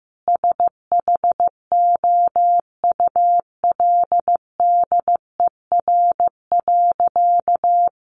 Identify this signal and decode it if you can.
Morse code, heard as a keyed tone, about 15 wpm, 700 Hz. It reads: SHOULDER.
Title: Additional Easy Phrases 15wpm